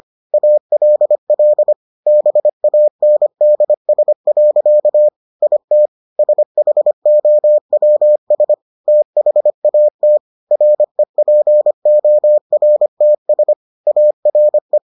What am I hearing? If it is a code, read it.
Morse code, heard as a keyed tone, about 25 wpm, 600 Hz. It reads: ALL BANDS. IT SHOWS THAT REPORTS ARE